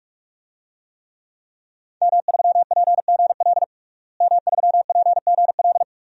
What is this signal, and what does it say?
Morse code, heard as a keyed tone, about 46 wpm, 700 Hz. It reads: M3PGL M3PGL
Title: DXCC-100 Call Signs - 5 Characters - Repeated 2x - 45wpm